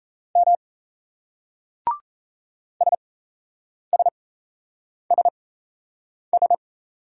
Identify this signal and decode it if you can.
Morse code, heard as a keyed tone, about 38 wpm, spaced as for 8 wpm, 700 Hz. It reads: M SSSS